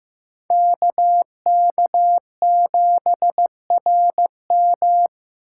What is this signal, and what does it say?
Morse code, heard as a keyed tone, about 15 wpm, 700 Hz. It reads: KK7RM
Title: USA Call Signs - 5 Characters - 3x Character Spacing - 15wpm